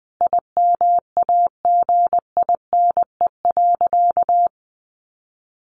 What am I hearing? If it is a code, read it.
Morse code, heard as a keyed tone, about 20 wpm, 700 Hz. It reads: IMAGINE.
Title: Sentences from Top 1400 Words 20wpm